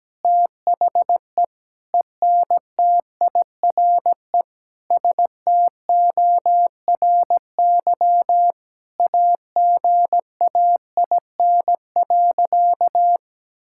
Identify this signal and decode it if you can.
Morse code, heard as a keyed tone, about 17 wpm, 700 Hz. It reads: THE ENTIRE STORY AGAIN.
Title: Sentences from Top 1000 Words - Review Part 5 - 17wpm